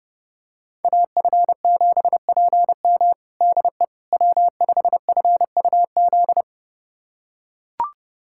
Morse code, heard as a keyed tone, about 32 wpm, 700 Hz. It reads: AF7PM DE W5FUZ
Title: USA Callsign DE Callsign - 5 Characters - 30wpm